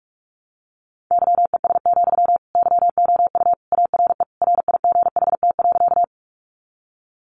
Morse code, heard as a keyed tone, about 44 wpm, 700 Hz. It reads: YES, YOU ARE RIGHT.
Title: Sentences from Top 500 Words 45wpm